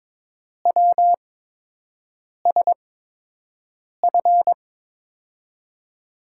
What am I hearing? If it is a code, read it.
Morse code, heard as a keyed tone, about 22 wpm, spaced as for 6 wpm, 700 Hz. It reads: WSF
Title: USA Call Signs - 5 Characters - 8x Character Spacing - 22wpm